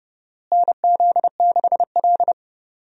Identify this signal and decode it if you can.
Morse code, heard as a keyed tone, about 31 wpm, 700 Hz. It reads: NZ6L
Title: USA Call Signs - 4 Characters - Repeated 2x - 30wpm